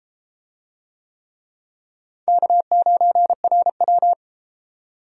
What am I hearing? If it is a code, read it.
Morse code, heard as a keyed tone, about 28 wpm, 700 Hz. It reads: K9RW